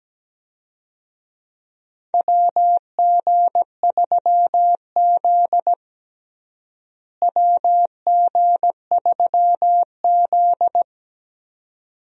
Morse code, heard as a keyed tone, about 17 wpm, 700 Hz. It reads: WG3Z WG3Z
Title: USA Call Signs - 4 Characters - Repeated 2x - 17wpm